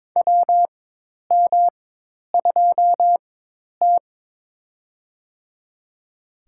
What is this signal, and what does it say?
Morse code, heard as a keyed tone, about 22 wpm, spaced as for 10 wpm, 700 Hz. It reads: WM2T
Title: USA Call Signs - 4 Characters - 4x Character Spacing - 22wpm